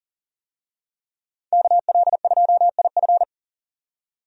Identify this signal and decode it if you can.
Morse code, heard as a keyed tone, about 39 wpm, 700 Hz. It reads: KL2IF
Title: USA Call Signs - 5 Characters - 40wpm